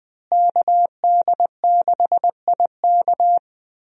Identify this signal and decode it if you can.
Morse code, heard as a keyed tone, about 20 wpm, 700 Hz. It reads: KD6IK
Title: USA Call Signs - 5 Characters - Repeated 2x - 20wpm